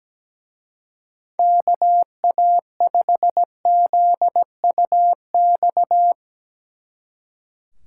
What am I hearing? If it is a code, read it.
Morse code, heard as a keyed tone, about 18 wpm, 700 Hz. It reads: KA5ZUX
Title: Speed Racing - USA 2x3 Call Signs 17wpm